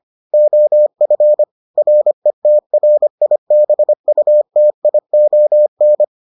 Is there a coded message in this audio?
Yes: Morse code from 0.3 to 6.0 s, heard as a keyed tone, about 25 wpm, 600 Hz. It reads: OF RETRIBUTION